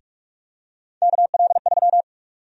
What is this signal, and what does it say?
Morse code, heard as a keyed tone, about 46 wpm, 700 Hz. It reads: KL3
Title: USA Call Sign Prefixes - Repeated 2x - 45wpm